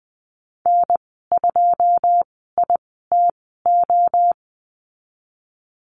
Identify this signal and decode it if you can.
Morse code, heard as a keyed tone, about 19 wpm, spaced as for 14 wpm, 700 Hz. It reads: N2ITO
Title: 082 Lesson 14g - Number 2 - Call Signs 20wpm